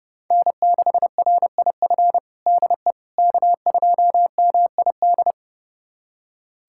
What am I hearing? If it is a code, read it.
Morse code, heard as a keyed tone, about 32 wpm, 700 Hz. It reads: N6RIF DE K2MID
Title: USA Callsign DE Callsign - 5 Characters - 30wpm